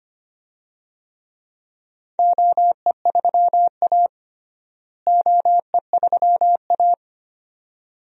Morse code, heard as a keyed tone, about 25 wpm, 700 Hz. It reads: OE3A OE3A
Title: DXCC-100 Call Signs - 4 Characters - Repeated 2x - 25wpm